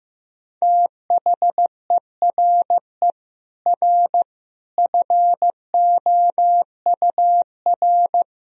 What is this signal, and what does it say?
Morse code, heard as a keyed tone, about 16 wpm, 700 Hz. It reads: THERE R FOUR
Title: Sentences from Top 400 Words and Abbreviations 15wpm